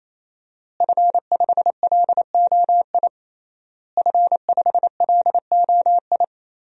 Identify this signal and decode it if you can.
Morse code, heard as a keyed tone, about 28 wpm, 700 Hz. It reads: F5LOI F5LOI
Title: DXCC-100 Call Signs - 5 Characters - Repeated 3x - 28wpm